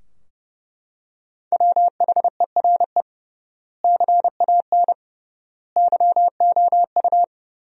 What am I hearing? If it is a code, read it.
Morse code, heard as a keyed tone, about 32 wpm, 700 Hz. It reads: WHERE CAN YOU
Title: 3x Word-Spacing - Sentences from Top 500 Words - 30wpm